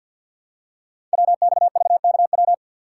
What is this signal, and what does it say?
Morse code, heard as a keyed tone, about 52 wpm, 700 Hz. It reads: WXVKW
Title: Code Groups - 5 Letters - 50wpm